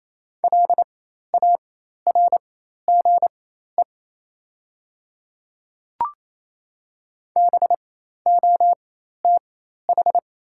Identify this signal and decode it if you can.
Morse code, heard as a keyed tone, about 28 wpm, spaced as for 13 wpm, 700 Hz. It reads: LARGE BOTH
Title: 4x Character Spacing - Top 500 Words - 28wpm